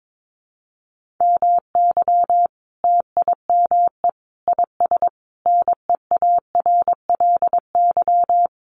Morse code, heard as a keyed tone, about 22 wpm, 700 Hz. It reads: MY TIME IS NEARLY
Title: Sentences from Top 1300 Words 22wpm